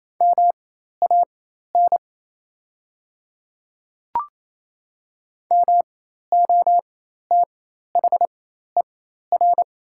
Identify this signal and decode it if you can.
Morse code, heard as a keyed tone, about 28 wpm, spaced as for 13 wpm, 700 Hz. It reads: MAN MOTHER
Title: 4x Character Spacing - Top 200 Words - 28wpm